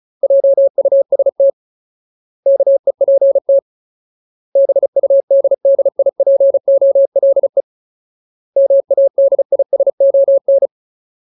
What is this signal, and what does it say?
Morse code, heard as a keyed tone, about 34 wpm, 550 Hz. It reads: JUST KEPT BUDDIPOLE MADISON
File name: Ultimate Word List / 35WPM Ultimate